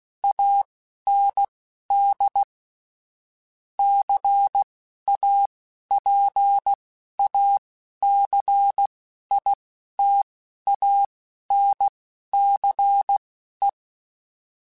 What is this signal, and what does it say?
Morse code, heard as a keyed tone, about 16 wpm, spaced as for 11 wpm, 800 Hz. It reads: AND CAPACITANCE